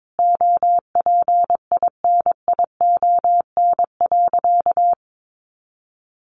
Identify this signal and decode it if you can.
Morse code, heard as a keyed tone, about 22 wpm, 700 Hz. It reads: OPINION.
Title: A Sound Of Thunder 22wpm